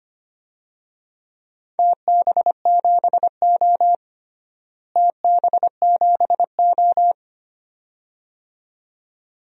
Morse code, heard as a keyed tone, about 25 wpm, 700 Hz. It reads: TB7O TB7O
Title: DXCC-100 Call Signs - 4 Characters - Repeated 2x - 25wpm